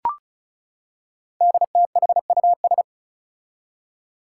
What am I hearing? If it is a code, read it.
Morse code, heard as a keyed tone, about 36 wpm, 700 Hz. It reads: DTHUS
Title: Five Challenging Letter-Number Combinations 35wpm